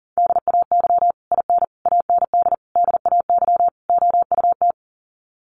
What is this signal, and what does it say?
Morse code, heard as a keyed tone, about 40 wpm, 700 Hz. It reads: DAY IN AND DAY OUT
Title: Sets of 5 Words Encore2 40wpm